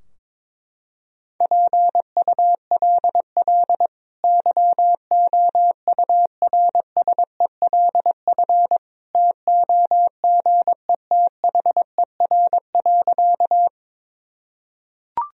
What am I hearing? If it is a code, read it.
Morse code, heard as a keyed tone, about 23 wpm, 700 Hz. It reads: PULL YOURSELF TOGETHER.
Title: Sentences from Top 1000 Words - Review Part 2 - 22wpm